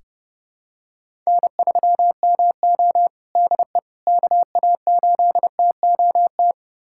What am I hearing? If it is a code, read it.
Morse code, heard as a keyed tone, about 30 wpm, 700 Hz. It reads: N3MO DE KA8TOT
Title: USA Callsign DE Callsign 30wpm